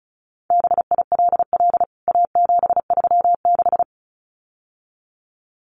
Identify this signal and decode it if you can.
Morse code, heard as a keyed tone, about 34 wpm, 700 Hz. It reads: BILL A736
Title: CWT Exchange 35wpm